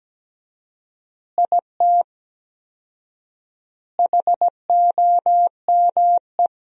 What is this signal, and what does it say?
Morse code, heard as a keyed tone, about 17 wpm, 700 Hz. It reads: IT HOME
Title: 4x Word-Spacing - Sentences from Top 500 Words - 17wpm